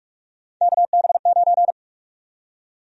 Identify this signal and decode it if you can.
Morse code, heard as a keyed tone, about 43 wpm, 700 Hz. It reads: KB9
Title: USA Call Sign Prefixes - 45wpm